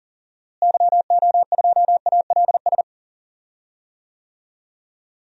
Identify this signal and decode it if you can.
Morse code, heard as a keyed tone, about 40 wpm, 700 Hz. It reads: YO2ALS
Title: DXCC-100 Call Signs - 6 Characters - Repeated 3x - 40wpm